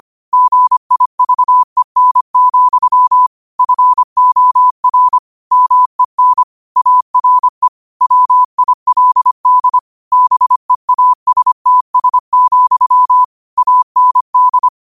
Morse code, heard as a keyed tone, about 25 wpm, 1000 Hz. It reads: GIVEN, FOR MEN ARE WILD BEASTS, AND